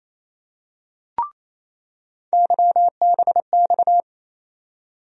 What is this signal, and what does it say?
Morse code, heard as a keyed tone, about 28 wpm, 700 Hz. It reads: YBX
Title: USA Call Sign Postfixes - 3 Letters 28wpm